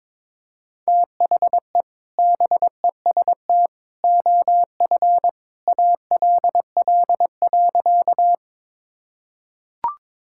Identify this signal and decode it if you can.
Morse code, heard as a keyed tone, about 22 wpm, 700 Hz. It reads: THE BEST OF ALL.
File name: practice / 22wpm / Sentences with 2-Word Phrases - 22wpm